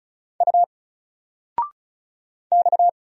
Morse code, heard as a keyed tone, about 35 wpm, 700 Hz. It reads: U X
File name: Single-Letter-Number-35wpm